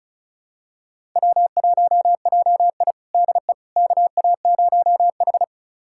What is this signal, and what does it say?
Morse code, heard as a keyed tone, about 36 wpm, 700 Hz. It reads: W1JI DE KA0H
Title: USA Callsign DE Callsign - 4 Characters - 35wpm